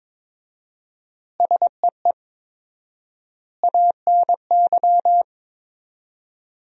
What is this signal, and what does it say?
Morse code, heard as a keyed tone, about 22 wpm, 700 Hz. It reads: SEE ANY
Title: 4x Word-Spacing - Sentences from Top 500 Words - 22wpm